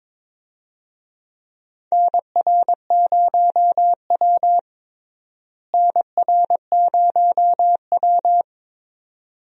Morse code, heard as a keyed tone, about 22 wpm, 700 Hz. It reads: NR0W NR0W
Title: USA Call Signs - 4 Characters - Repeated 2x - 22wpm